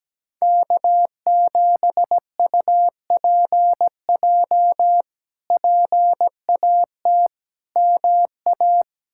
Morse code, heard as a keyed tone, about 17 wpm, 700 Hz. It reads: K7UPJ PAT MA
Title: K1USN SST QSO Practice - 17wpm